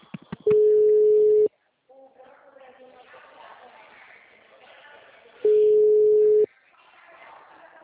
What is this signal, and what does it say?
Ringback tone (Europe, 425 Hz), with speech.